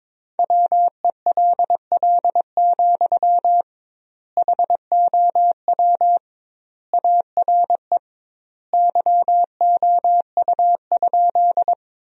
Morse code, heard as a keyed tone, about 22 wpm, 700 Hz. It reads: WELL, HOW ARE YOU?
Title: 2x Word-Spacing - Sentences from Top 200 Words - 22wpm